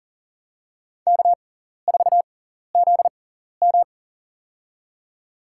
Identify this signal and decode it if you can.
Morse code, heard as a keyed tone, about 40 wpm, spaced as for 14 wpm, 700 Hz. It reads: K4ZM
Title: USA Call Signs - 4 Characters - 6x Character Spacing - 40wpm